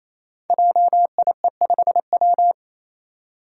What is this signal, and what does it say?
Morse code, heard as a keyed tone, about 29 wpm, 700 Hz. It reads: JIE5W
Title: Code Groups - 5 Letter:Numbers - 28wpm